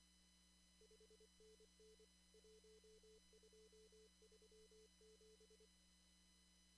Morse code, heard as a keyed tone, about 24 wpm, 440 Hz. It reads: 5NN 1237
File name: CW / kt5j wpx cw-op-k5pi